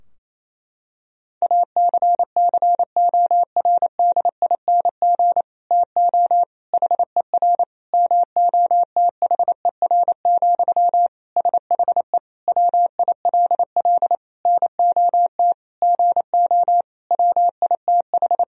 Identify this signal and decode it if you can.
Morse code, heard as a keyed tone, about 28 wpm, 700 Hz. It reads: ACCORDING TO HER MOTHER, SHE WILL NOT GO WITH
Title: Sentences from Top 1000 Words - Review Part 4 - 28wpm